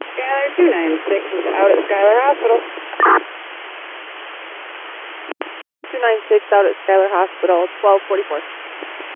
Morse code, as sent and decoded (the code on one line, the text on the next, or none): none